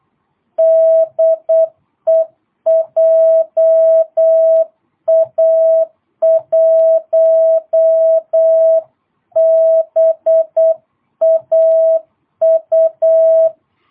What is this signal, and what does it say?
Morse code, heard as a keyed tone, about 8 wpm, 660 Hz. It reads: DE JA1BAU
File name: jh8cbh / 2022cw